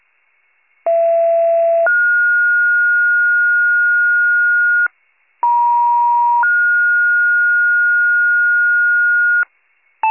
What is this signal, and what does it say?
Two-tone page, heard as a one-second tone, then a three-second tone, about 670 then 1430 Hz, about 950 then 1430 Hz. It is heard from 0.9 to 9.4 s.